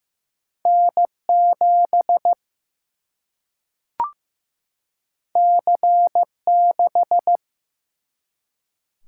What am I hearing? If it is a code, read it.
Morse code, heard as a keyed tone, about 15 wpm, 700 Hz. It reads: N 7 C 6
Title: Single-Letter Single-Number 15wpm